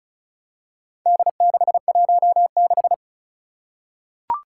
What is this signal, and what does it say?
Morse code, heard as a keyed tone, about 36 wpm, 700 Hz. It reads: D616
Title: Four Challenging Letter-Number Combinations 35wpm